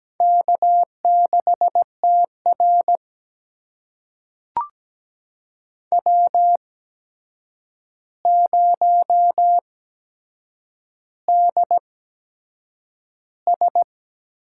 Morse code, heard as a keyed tone, about 17 wpm, spaced as for 7 wpm, 700 Hz. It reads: K6TR W0DS
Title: USA Call Signs - 4 Characters - 8x Character Spacing - 17wpm